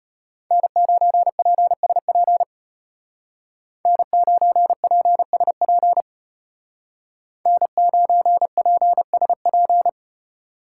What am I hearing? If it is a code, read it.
Morse code, heard as a keyed tone, about 31 wpm, 700 Hz. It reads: N9PSP N9PSP N9PSP